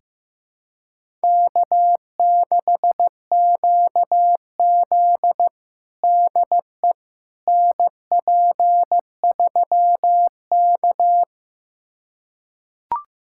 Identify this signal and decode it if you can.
Morse code, heard as a keyed tone, about 15 wpm, 700 Hz. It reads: K6QZ DE NP3K
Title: USA Callsign DE Callsign - 4 Characters - 15wpm